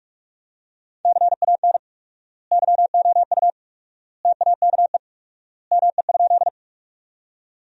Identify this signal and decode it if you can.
Morse code, heard as a keyed tone, about 47 wpm, 700 Hz. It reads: CAN YOU TAKE ME?
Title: 4x Word-Spacing - Sentences from Top 200 Words - 45wpm